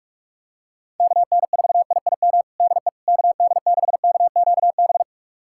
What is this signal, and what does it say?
Morse code, heard as a keyed tone, about 47 wpm, 700 Hz. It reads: KN4IIM DE KD6KQB
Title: USA Callsign DE Callsign 45wpm